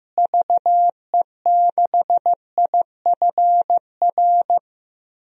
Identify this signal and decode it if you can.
Morse code, heard as a keyed tone, about 15 wpm, 700 Hz. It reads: VE6IFR
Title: DXCC-100 Call Signs - 6 Characters - Repeated 3x - 15wpm